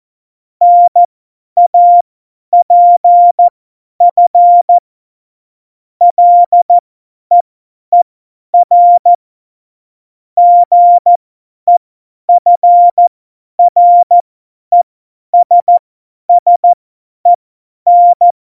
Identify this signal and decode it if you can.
Morse code, heard as a keyed tone, about 14 wpm, spaced as for 10 wpm, 700 Hz. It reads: NAPF LEER GEFRESSEN
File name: Morsekurs / Witze / Witz 01. August 17, Tempo 50 F70